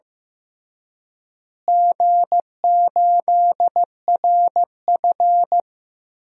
Morse code, heard as a keyed tone, about 15 wpm, 700 Hz. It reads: G8RF
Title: DXCC-100 Call Signs - 4 Characters - Repeated 3x - 15wpm